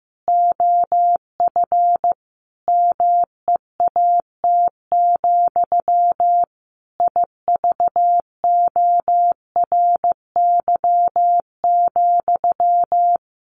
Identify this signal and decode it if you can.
Morse code, heard as a keyed tone, about 15 wpm, 700 Hz. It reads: OF MEAT, IVORY,
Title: A Sound Of Thunder 15wpm